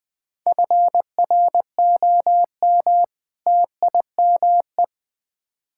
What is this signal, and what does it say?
Morse code, heard as a keyed tone, about 20 wpm, 700 Hz. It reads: FROM TIME
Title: Build A Phrase - 4-Word Phrases - 20wpm